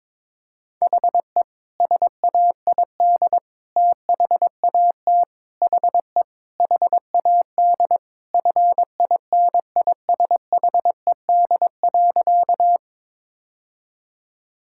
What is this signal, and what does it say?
Morse code, heard as a keyed tone, about 22 wpm, 700 Hz. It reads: HE SAID THAT HE HAD FINISHED.
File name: practice / 22wpm / Sentences with 3-Word Phrases - 22wpm